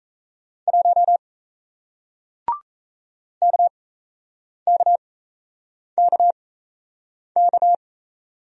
Morse code, heard as a keyed tone, about 38 wpm, spaced as for 8 wpm, 700 Hz. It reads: 1 KKKK